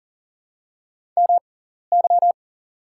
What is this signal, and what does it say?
Morse code, heard as a keyed tone, about 40 wpm, spaced as for 14 wpm, 700 Hz. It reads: MY
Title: 6x Character Spacing - Top 100 Words - 40wpm